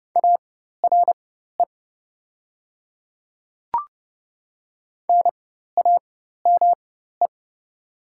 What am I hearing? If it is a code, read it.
Morse code, heard as a keyed tone, about 30 wpm, spaced as for 14 wpm, 700 Hz. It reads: ARE NAME
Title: 4x Character Spacing - Top 200 Words - 30wpm